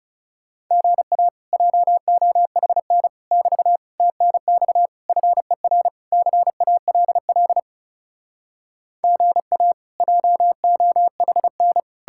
Morse code, heard as a keyed tone, about 37 wpm, 700 Hz. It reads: GA JOHN <BT> TNX FER CALL GA JOHN
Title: Speed-Racing Common QSO Phrases Encore - 30wpm